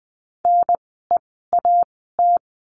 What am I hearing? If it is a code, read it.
Morse code, heard as a keyed tone, about 20 wpm, spaced as for 15 wpm, 700 Hz. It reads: NEAT
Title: 054 Lesson 10f - Letter r - Words 20wpm